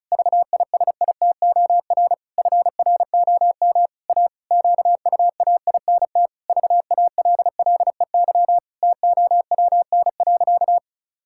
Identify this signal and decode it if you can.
Morse code, heard as a keyed tone, about 34 wpm, 700 Hz. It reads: VISITOR FROM A QUAINT VALLEY TOWN.
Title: Pangram Challenge v2 - 35wpm